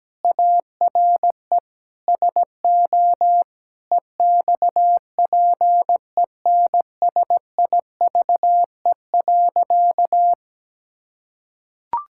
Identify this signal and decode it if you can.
Morse code, heard as a keyed tone, about 17 wpm, 700 Hz. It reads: ARE SO EXPENSIVE.
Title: Sentences from Top 1000 Words - Review Part 6 - 17wpm